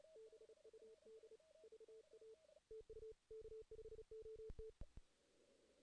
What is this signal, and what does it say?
Morse code, heard as a keyed tone, about 34 wpm, 440 Hz. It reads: BUD VA TU K5OT
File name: 2016 JAN CW NAQP / 2016 Jan naqp k5ot - 0-2